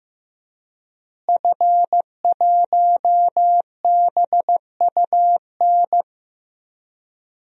Morse code, heard as a keyed tone, about 15 wpm, 700 Hz. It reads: F1BUN
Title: DXCC-100 Call Signs - 5 Characters - Repeated 3x - 15wpm